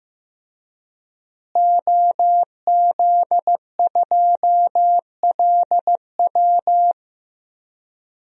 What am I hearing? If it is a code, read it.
Morse code, heard as a keyed tone, about 15 wpm, 700 Hz. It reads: OZ2LW
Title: DXCC-100 Call Signs - 5 Characters - Repeated 3x - 15wpm